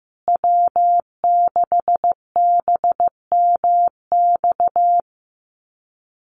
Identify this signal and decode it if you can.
Morse code, heard as a keyed tone, about 14 wpm, 700 Hz. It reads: W6BMX
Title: USA Call Signs Repeated 3x 15wpm